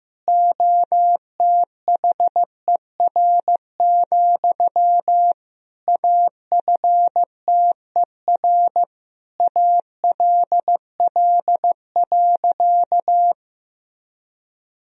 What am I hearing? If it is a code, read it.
Morse code, heard as a keyed tone, about 15 wpm, 700 Hz. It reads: OTHER, AFTER ALL.
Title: Sentences from Top 1000 Words - Review Part 1 - 15wpm